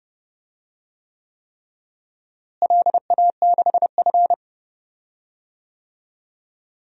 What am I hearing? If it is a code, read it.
Morse code, heard as a keyed tone, about 30 wpm, 700 Hz. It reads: LA6F
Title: DXCC-100 Call Signs - 4 Characters - Repeated 3x - 30wpm